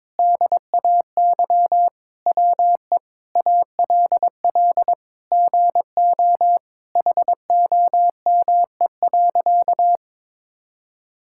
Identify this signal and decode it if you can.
Morse code, heard as a keyed tone, about 22 wpm, 700 Hz. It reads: DAY WE ALL GO HOME.
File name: practice / 22wpm / Sentences with 4-Word Phrases - 22wpm